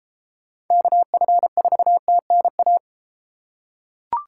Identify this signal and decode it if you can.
Morse code, heard as a keyed tone, about 32 wpm, 700 Hz. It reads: KF4TNA